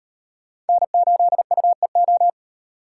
Morse code, heard as a keyed tone, about 38 wpm, 700 Hz. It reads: N8UEO